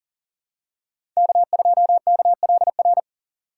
Morse code, heard as a keyed tone, about 37 wpm, 700 Hz. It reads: K2KLR
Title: USA Call Signs - 5 Characters - 4x Character Spacing - 40wpm